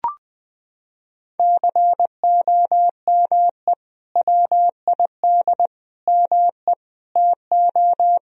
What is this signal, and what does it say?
Morse code, heard as a keyed tone, about 20 wpm, 700 Hz. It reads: COME WID ME TO
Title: Sentences from Top 200 Words and Abbreviations 20wpm